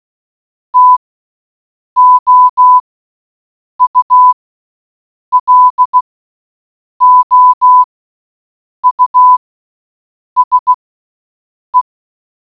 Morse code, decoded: TOULOUSE